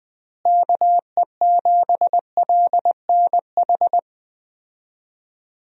Morse code, heard as a keyed tone, about 20 wpm, 700 Hz. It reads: KE7LNH
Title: USA Call Signs - 6 Characters - Repeated 3x - 20wpm